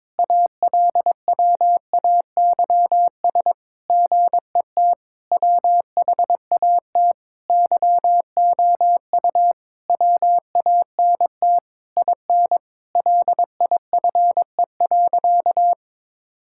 Morse code, heard as a keyed tone, about 22 wpm, 700 Hz. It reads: ALWAYS GET WHAT YOU WANT IN LIFE.
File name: Sentences with 7-Word Phrases - 22wpm